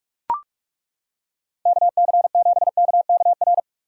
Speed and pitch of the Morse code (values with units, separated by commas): 38 wpm, 700 Hz